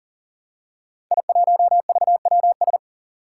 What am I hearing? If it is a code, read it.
Morse code, heard as a keyed tone, about 41 wpm, 700 Hz. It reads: I1VWS
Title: Five Challenging Letter-Number Combinations 40wpm